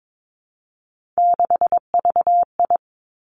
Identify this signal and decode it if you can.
Morse code, heard as a keyed tone, about 22 wpm, 700 Hz. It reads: 6VI
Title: Three Challenging Letter-Number Combinations 22wpm